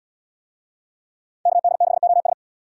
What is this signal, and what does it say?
Morse code, heard as a keyed tone, about 77 wpm, 700 Hz. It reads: KD6OS